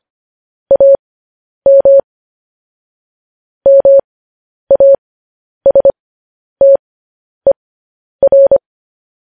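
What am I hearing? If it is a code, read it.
Morse code, heard as a keyed tone, about 25 wpm, spaced as for 10 wpm, 550 Hz. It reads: AM MASTER
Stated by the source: Morse code transmission of news headlines.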